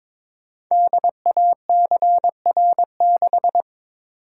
Morse code, heard as a keyed tone, about 22 wpm, 700 Hz. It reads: DACR6
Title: Code Groups - 5 Letter:Numbers - 22wpm